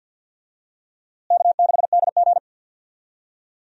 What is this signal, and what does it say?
Morse code, heard as a keyed tone, about 52 wpm, 700 Hz. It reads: K6DG
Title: USA Call Signs - 4 Characters - Repeated 3x - 50wpm